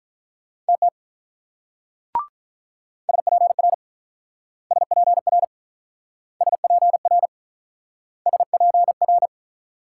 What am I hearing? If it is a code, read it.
Morse code, heard as a keyed tone, about 44 wpm, 700 Hz. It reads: TT SPR SPR SPR SPR